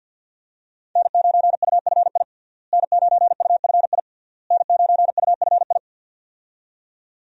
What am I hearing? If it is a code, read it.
Morse code, heard as a keyed tone, about 46 wpm, 700 Hz. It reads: N9UFI N9UFI N9UFI
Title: USA Call Signs - 5 Characters - Repeated 3x - 50wpm